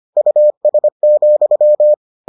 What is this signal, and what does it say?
Morse code, heard as a keyed tone, about 25 wpm, 600 Hz. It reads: US,